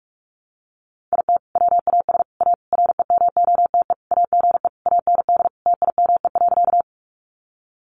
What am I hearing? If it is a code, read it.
Morse code, heard as a keyed tone, about 44 wpm, 700 Hz. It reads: IT WAS A REMOTE AGE AND TIME.
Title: Sentences from Top 1300 Words 45wpm